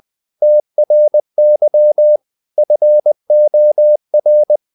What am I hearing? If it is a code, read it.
Morse code, heard as a keyed tone, about 20 wpm, 600 Hz. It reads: TRY FOR
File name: the old man and the sea part 2 20wpm